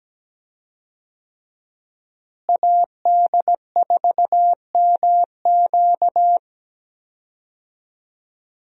Morse code, heard as a keyed tone, about 17 wpm, 700 Hz. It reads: AD4MQ
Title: USA Call Signs - 5 Characters - Repeated 3x - 17wpm